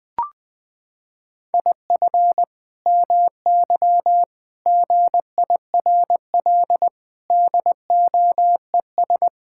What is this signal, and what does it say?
Morse code, heard as a keyed tone, about 20 wpm, 700 Hz. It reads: IF MY GIRL DOES
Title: Sentences from Top 1600 Words 20wpm